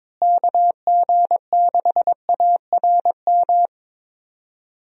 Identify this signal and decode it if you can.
Morse code, heard as a keyed tone, about 22 wpm, 700 Hz. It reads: KG6ARM
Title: USA Call Signs - 6 Characters - Repeated 2x - 22wpm